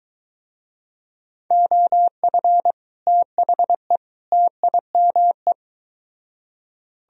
Morse code, heard as a keyed tone, about 23 wpm, 700 Hz. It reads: OF THE TIME